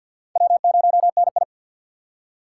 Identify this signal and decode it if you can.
Morse code, heard as a keyed tone, about 52 wpm, 700 Hz. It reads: W0NI
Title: USA Call Signs - 4 Characters - 2x Character Spacing - 50wpm